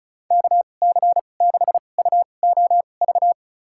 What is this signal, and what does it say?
Morse code, heard as a keyed tone, about 34 wpm, spaced as for 25 wpm, 700 Hz. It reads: KC6UOV
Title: USA Call Signs - 6 Characters - 2x Character Spacing - 35wpm